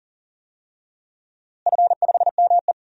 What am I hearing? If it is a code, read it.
Morse code, heard as a keyed tone, about 40 wpm, 700 Hz. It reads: F5ME
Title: DXCC-100 Call Signs - 4 Characters - Repeated 3x - 40wpm